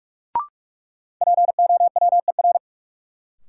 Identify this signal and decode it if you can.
Morse code, heard as a keyed tone, about 46 wpm, 700 Hz. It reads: POWER
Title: Sentences from Top 500 Words - New Word Drill - 45wpm